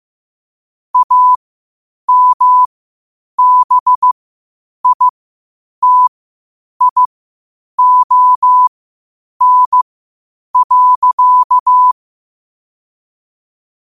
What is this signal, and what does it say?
Morse code, heard as a keyed tone, about 15 wpm, spaced as for 8 wpm, 1000 Hz. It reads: AMBITION.
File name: qotdfiles / 8wpm